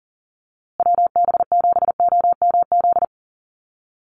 Morse code, heard as a keyed tone, about 40 wpm, 700 Hz. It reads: WB7OMZ
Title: USA Call Signs Repeated 2x 40wpm